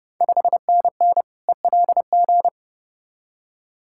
Morse code, heard as a keyed tone, about 30 wpm, 690 Hz. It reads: 5NN ELG
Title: Ontario QSO Party Exchange 30wpm